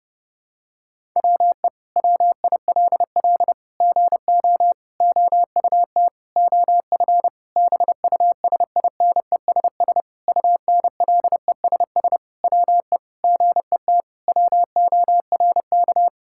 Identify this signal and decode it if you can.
Morse code, heard as a keyed tone, about 30 wpm, 700 Hz. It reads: WE WILL GO OUT OF BUSINESS UNLESS WE GET WORK
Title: Sentences from Top 1000 Words - Review Part 6 - 30wpm